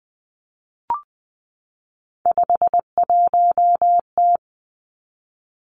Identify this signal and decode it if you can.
Morse code, heard as a keyed tone, about 20 wpm, 700 Hz. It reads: 51T
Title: Three Challenging Letter-Number Combinations 20wpm